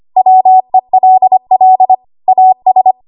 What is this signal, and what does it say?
Morse code, heard as a keyed tone, about 25 wpm, 750 Hz. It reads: WELL AS
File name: Morse / 25WPM